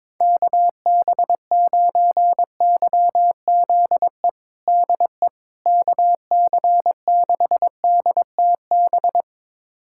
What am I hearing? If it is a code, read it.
Morse code, heard as a keyed tone, about 22 wpm, 700 Hz. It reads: KB9YZE DE KC6DTB
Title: USA Callsign DE Callsign - 6 Characters - 22wpm